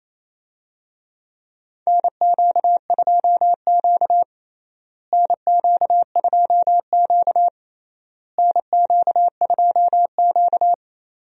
Morse code, heard as a keyed tone, about 27 wpm, 700 Hz. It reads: NQ2Q NQ2Q NQ2Q
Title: USA Call Signs - 4 Characters - Repeated 3x - 28wpm